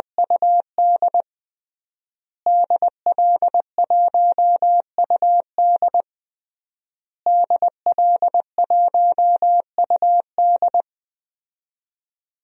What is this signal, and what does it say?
Morse code, heard as a keyed tone, about 20 wpm, 700 Hz. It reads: UD DL1UD DL1UD
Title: DXCC-100 Call Signs - 5 Characters - Repeated 3x - 20wpm